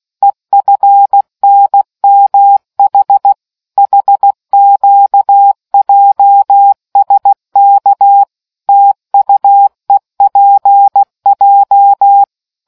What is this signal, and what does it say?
Morse code, heard as a keyed tone, about 16 wpm, 800 Hz. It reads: EFNMH HQJSK TUEPJ